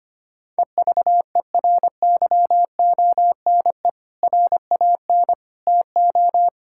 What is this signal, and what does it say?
Morse code, heard as a keyed tone, about 25 wpm, 700 Hz. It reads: EVERYONE RAN TO
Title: Sentences from Top 1600 Words 25wpm